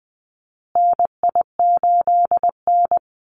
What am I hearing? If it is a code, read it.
Morse code, heard as a keyed tone, about 20 wpm, 700 Hz. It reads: NI8N
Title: Call Signs Repeated 2x - 20wpm